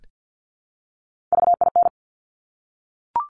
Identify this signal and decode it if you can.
Morse code, heard as a keyed tone, about 50 wpm, 700 Hz. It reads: VIN
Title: Top Three Letter Combinations in Words 50wpm